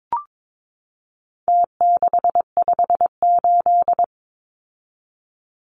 Morse code, heard as a keyed tone, about 22 wpm, 700 Hz. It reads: T658
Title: Numbers - Contest Style - 4-Digit 22wpm